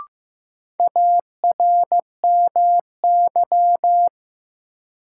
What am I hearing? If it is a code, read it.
Morse code, heard as a keyed tone, about 15 wpm, 700 Hz. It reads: ARMY
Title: Sentences from Top 800 Words - New Word Drill - 15wpm